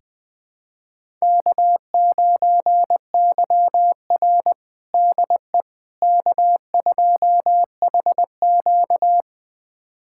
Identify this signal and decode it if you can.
Morse code, heard as a keyed tone, about 20 wpm, 700 Hz. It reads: K9YR DE K2HQ
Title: USA Callsign DE Callsign - 4 Characters - 20wpm